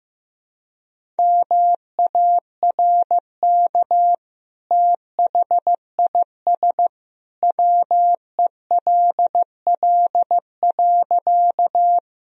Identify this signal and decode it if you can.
Morse code, heard as a keyed tone, about 15 wpm, 700 Hz. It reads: MARK THIS WELL.
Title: Sentences from Top 1000 Words - Review Part 1 - 15wpm